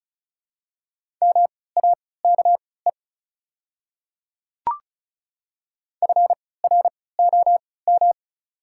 Morse code, heard as a keyed tone, about 36 wpm, spaced as for 20 wpm, 690 Hz. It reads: MAKE FROM
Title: 3x Character Spacing - Top 100 Words - 35wpm